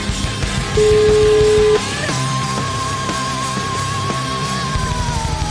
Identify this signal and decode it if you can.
Ringback tone (Europe, 425 Hz), with noise.